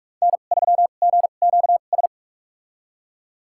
Morse code, heard as a keyed tone, about 46 wpm, spaced as for 32 wpm, 700 Hz. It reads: N3GQS
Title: USA Call Signs - 5 Characters - 2x Character Spacing - 45wpm